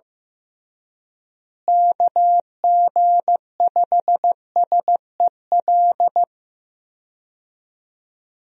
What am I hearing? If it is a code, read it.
Morse code, heard as a keyed tone, about 16 wpm, 700 Hz. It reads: KG5SEL
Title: USA Call Signs - 6 Characters - Repeated 3x - 15wpm